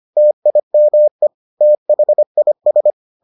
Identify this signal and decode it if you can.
Morse code, heard as a keyed tone, about 25 wpm, 600 Hz. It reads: TIME THIS